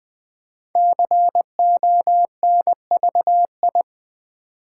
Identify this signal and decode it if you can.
Morse code, heard as a keyed tone, about 20 wpm, 700 Hz. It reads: CONVI
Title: Build A Word - 5 Letters At A Time - 10-Letter Words - 20wpm